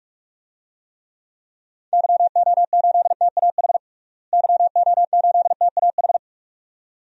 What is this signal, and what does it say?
Morse code, heard as a keyed tone, about 46 wpm, 700 Hz. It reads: YO8TAH YO8TAH
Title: DXCC-100 Call Signs - 6 Characters - Repeated 2x - 45wpm